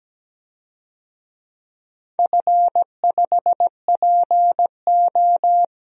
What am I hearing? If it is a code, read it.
Morse code, heard as a keyed tone, about 17 wpm, 700 Hz. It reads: F5PO
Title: DXCC-100 Call Signs - 4 Characters - 17wpm